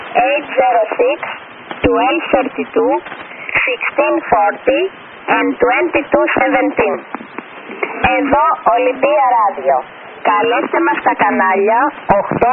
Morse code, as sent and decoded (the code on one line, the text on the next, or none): none